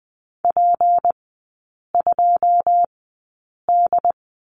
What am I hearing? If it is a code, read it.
Morse code, heard as a keyed tone, about 20 wpm, spaced as for 8 wpm, 690 Hz. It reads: P2D
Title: 159 Lesson 26e - Letter g - Three Characters 20wpm